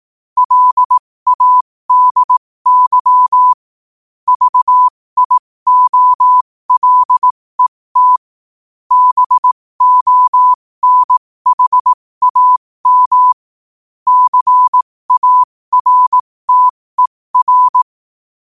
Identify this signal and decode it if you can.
Morse code, heard as a keyed tone, about 18 wpm, spaced as for 15 wpm, 1000 Hz. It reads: LADY VIOLET BONHAM CARTER
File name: qotdfiles / 15wpm